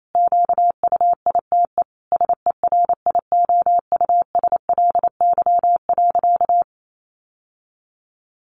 Morse code, heard as a keyed tone, about 28 wpm, 700 Hz. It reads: QUITE SERIOUSLY.
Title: Sentences from Top 1200 Words 28wpm